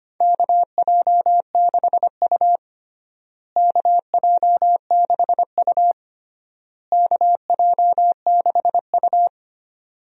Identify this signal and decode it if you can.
Morse code, heard as a keyed tone, about 25 wpm, 700 Hz. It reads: KJ6U KJ6U KJ6U
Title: USA Call Signs - 4 Characters - Repeated 3x - 25wpm